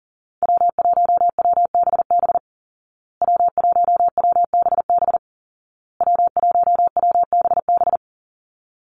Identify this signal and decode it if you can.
Morse code, heard as a keyed tone, about 40 wpm, 700 Hz. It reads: W1WBB W1WBB W1WBB
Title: Call Signs Repeated 2x - 40wpm